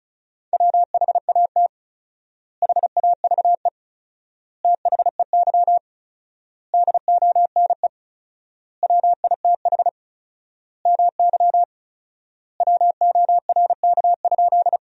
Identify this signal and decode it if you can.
Morse code, heard as a keyed tone, about 36 wpm, 700 Hz. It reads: WHAT HAVE THEY DONE WITH MY WORK?
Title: 4x Word-Spacing - Sentences from Top 500 Words - 35wpm